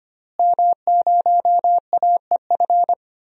Morse code, heard as a keyed tone, about 25 wpm, 700 Hz. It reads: M0AEF
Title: DXCC-100 Call Signs - 5 Characters - Repeated 2x - 25wpm